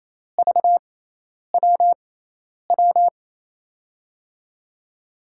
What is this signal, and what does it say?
Morse code, heard as a keyed tone, about 27 wpm, spaced as for 10 wpm, 700 Hz. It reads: VWW
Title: USA Call Signs - 6 Characters - 6x Character Spacing - 28wpm